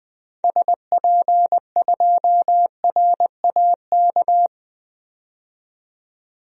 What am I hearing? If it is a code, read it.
Morse code, heard as a keyed tone, about 20 wpm, 700 Hz. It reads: SP2RAK
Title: DXCC-100 Call Signs - 6 Characters - 20wpm